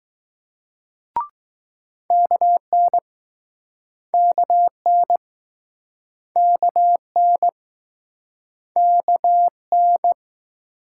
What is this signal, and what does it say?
Morse code, heard as a keyed tone, about 19 wpm, 700 Hz. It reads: KN KN KN KN